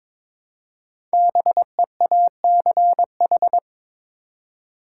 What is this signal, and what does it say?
Morse code, heard as a keyed tone, about 22 wpm, 700 Hz. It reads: BEACH
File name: 5 Letter Words - First 3 Letters Given - 22wpm